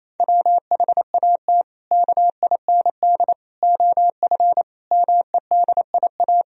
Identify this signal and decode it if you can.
Morse code, heard as a keyed tone, about 28 wpm, 700 Hz. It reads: WHAT KIND OF MEDIA
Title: Sentences from Top 1000 Words - Review Part 4 - 28wpm